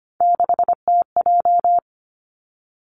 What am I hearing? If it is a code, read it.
Morse code, heard as a keyed tone, about 25 wpm, 700 Hz. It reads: 6TJ
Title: Three Challenging Letter-Number Combinations 25wpm